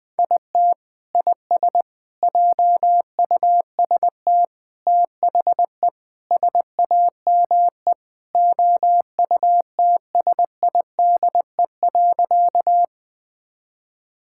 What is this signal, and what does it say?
Morse code, heard as a keyed tone, about 20 wpm, 700 Hz. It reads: IT IS JUST THE SAME OUTSIDE.
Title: Sentences from Top 1000 Words - Review Part 6 - 20wpm